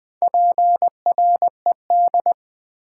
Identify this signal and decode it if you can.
Morse code, heard as a keyed tone, about 20 wpm, 700 Hz. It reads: PRED
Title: Build A Word - 4 Letters At A Time - 10-Letter Words - 20wpm